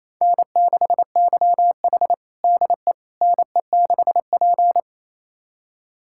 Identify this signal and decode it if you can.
Morse code, heard as a keyed tone, about 28 wpm, 700 Hz. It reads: N6YH DE NE6P
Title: USA Callsign DE Callsign - 4 Characters - 28wpm